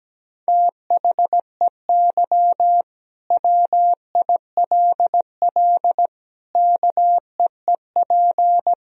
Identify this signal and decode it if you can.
Morse code, heard as a keyed tone, about 17 wpm, 700 Hz. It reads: THEY WILL KEEP
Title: Sentences from Top 1000 Words - Review Part 3 - 17wpm